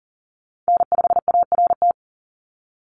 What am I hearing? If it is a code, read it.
Morse code, heard as a keyed tone, about 40 wpm, 690 Hz. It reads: N5ART
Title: USA Call Signs 40wpm